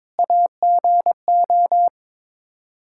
Morse code, heard as a keyed tone, about 21 wpm, 700 Hz. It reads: AGO
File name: practice / 22wpm / Sets of 6 Words - 22wpm